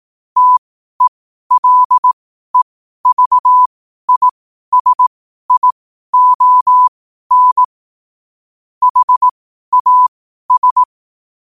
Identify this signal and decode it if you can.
Morse code, heard as a keyed tone, about 18 wpm, spaced as for 12 wpm, 1000 Hz. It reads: TELEVISION HAS
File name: qotdfiles / 12wpm